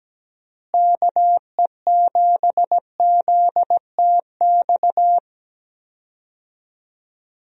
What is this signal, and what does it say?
Morse code, heard as a keyed tone, about 17 wpm, 700 Hz. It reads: KE7ZTX
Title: USA Call Signs - 6 Characters - Repeated 3x - 17wpm